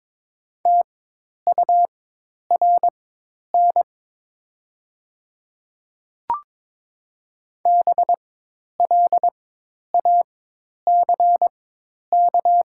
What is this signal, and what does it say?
Morse code, heard as a keyed tone, about 22 wpm, spaced as for 10 wpm, 700 Hz. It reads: TURN BLACK
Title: 4x Character Spacing - Top 500 Words - 22wpm